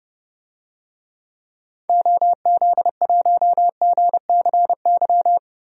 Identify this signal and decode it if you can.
Morse code, heard as a keyed tone, about 32 wpm, 700 Hz. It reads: OZ1GCY
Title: DXCC-100 Call Signs - 6 Characters - Repeated 3x - 30wpm